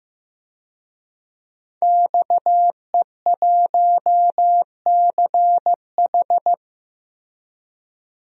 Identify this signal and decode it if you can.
Morse code, heard as a keyed tone, about 15 wpm, 700 Hz. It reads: XE1CH
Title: DXCC-100 Call Signs - 5 Characters - 15wpm